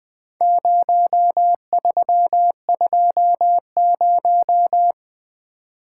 Morse code, heard as a keyed tone, about 20 wpm, 700 Hz. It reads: 0320
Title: Four Letter-Number Combinations - 20wpm